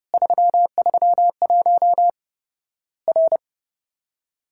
Morse code, heard as a keyed tone, about 30 wpm, 700 Hz. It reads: 331 R
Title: Worked All Europe DX Contest - QTC Exchange - 30wpm